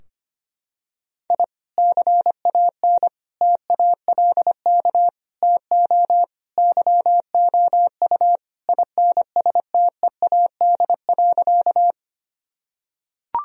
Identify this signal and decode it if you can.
Morse code, heard as a keyed tone, about 25 wpm, 700 Hz. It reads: I CAN TALK TO YOU INSTEAD.
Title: Sentences from Top 1000 Words - Review Part 4 - 25wpm